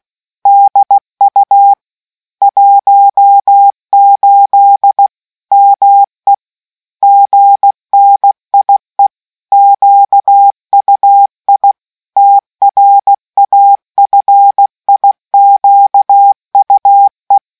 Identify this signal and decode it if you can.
Morse code, heard as a keyed tone, about 16 wpm, 800 Hz. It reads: DU 18 ME GNIE QUI TRAFIQUE